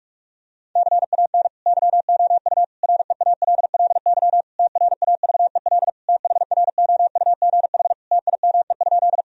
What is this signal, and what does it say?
Morse code, heard as a keyed tone, about 46 wpm, 700 Hz. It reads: CAN YOU REALLY TRAVEL THROUGH TIME?
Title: Sentences from Top 1000 Words - Review Part 2 - 45wpm